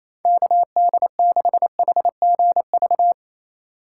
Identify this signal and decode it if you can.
Morse code, heard as a keyed tone, about 28 wpm, 700 Hz. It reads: KD6HGV
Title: USA Call Signs - 6 Characters - 3x Character Spacing - 28wpm